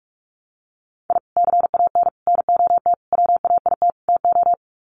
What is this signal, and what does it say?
Morse code, heard as a keyed tone, about 44 wpm, 700 Hz. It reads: I CAN NOT WAIT TO
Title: Sets of 5 Words Encore 45wpm